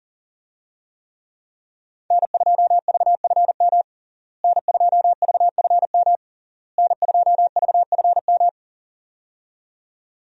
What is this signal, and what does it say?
Morse code, heard as a keyed tone, about 40 wpm, 700 Hz. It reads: N2VFM N2VFM N2VFM
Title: USA Call Signs - 5 Characters - Repeated 3x - 40wpm